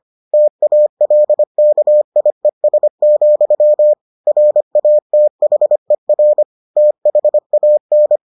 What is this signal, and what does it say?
Morse code, heard as a keyed tone, about 25 wpm, 600 Hz. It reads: TALKIES, RATHER THAN